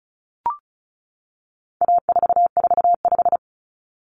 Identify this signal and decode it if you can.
Morse code, heard as a keyed tone, about 34 wpm, 700 Hz. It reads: A445
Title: Common Cut Numbers - 4-Digit 35wpm